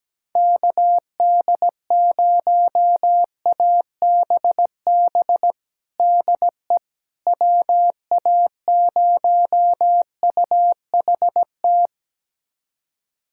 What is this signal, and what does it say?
Morse code, heard as a keyed tone, about 17 wpm, 700 Hz. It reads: KD0ABB DE WA0UHT
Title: USA Callsign DE Callsign - 6 Characters - 17wpm